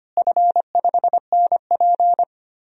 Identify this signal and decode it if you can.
Morse code, heard as a keyed tone, about 25 wpm, 700 Hz. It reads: F5NP
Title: DXCC-100 Call Signs - 4 Characters - Repeated 3x - 25wpm